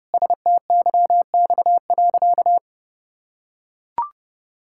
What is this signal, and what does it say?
Morse code, heard as a keyed tone, about 30 wpm, 700 Hz. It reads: STYX.
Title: Pangram Challenge v2 - 30wpm